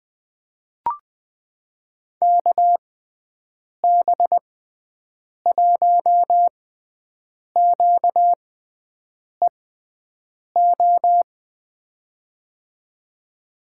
Morse code, heard as a keyed tone, about 21 wpm, spaced as for 7 wpm, 700 Hz. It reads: KB1QEO
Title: USA Call Signs - 6 Characters - 6x Character Spacing - 20wpm